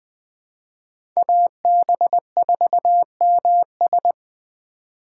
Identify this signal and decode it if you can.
Morse code, heard as a keyed tone, about 20 wpm, 700 Hz. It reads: AB4MS